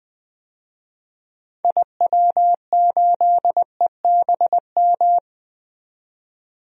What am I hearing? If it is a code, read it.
Morse code, heard as a keyed tone, about 21 wpm, 700 Hz. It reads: IW8EBM
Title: DXCC-100 Call Signs - 6 Characters - 20wpm